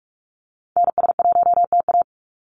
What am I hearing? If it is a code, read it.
Morse code, heard as a keyed tone, about 44 wpm, 700 Hz. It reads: NS1TA
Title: NA SOTA Activator Callsigns 45wpm